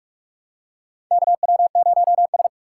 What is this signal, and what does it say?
Morse code, heard as a keyed tone, about 46 wpm, 700 Hz. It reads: KW0S
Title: USA Call Signs - 4 Characters - 3x Character Spacing - 45wpm